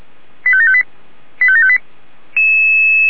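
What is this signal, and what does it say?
Five-tone selective call (ZVEI), heard as quick runs of five tones, tones of about 70 ms. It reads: 85758 85758
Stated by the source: ZVEI five-tone call received on an SDR.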